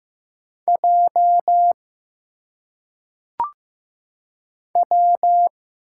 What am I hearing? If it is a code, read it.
Morse code, heard as a keyed tone, about 15 wpm, 700 Hz. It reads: J W
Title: Single Letters - Phonetic Pronunciation 15wpm